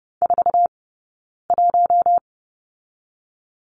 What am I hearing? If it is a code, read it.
Morse code, heard as a keyed tone, about 30 wpm, 700 Hz. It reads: 4 1
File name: Lesson 20d - Number 3 - Two Characters 30wpm